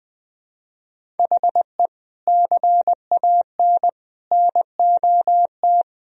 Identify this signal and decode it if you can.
Morse code, heard as a keyed tone, about 20 wpm, 700 Hz. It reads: HE CAN NOT
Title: Sentences from Top 1000 Words - Review Part 1 - 20wpm